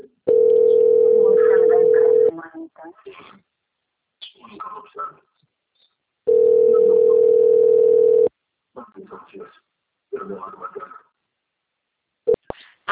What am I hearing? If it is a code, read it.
Ringback tone (North America), 440 + 480 Hz, with speech.
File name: out- - - - - .17150